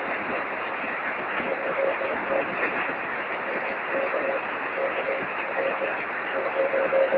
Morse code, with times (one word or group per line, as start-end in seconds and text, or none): none